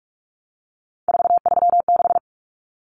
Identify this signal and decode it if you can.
Morse code, heard as a keyed tone, about 44 wpm, 700 Hz. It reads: 436
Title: Numbers - 3-Digit 45wpm